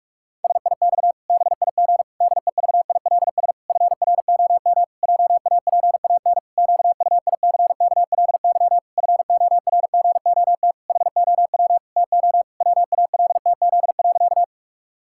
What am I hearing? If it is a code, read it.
Morse code, heard as a keyed tone, about 46 wpm, 700 Hz. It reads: SIX BIG DEVILS FROM JAPAN QUICKLY FORGOT HOW TO WALTZ.
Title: Pangram Challenge v2 - 45wpm